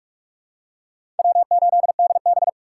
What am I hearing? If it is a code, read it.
Morse code, heard as a keyed tone, about 46 wpm, 700 Hz. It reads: W8DB
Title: USA Call Signs - 4 Characters - 3x Character Spacing - 45wpm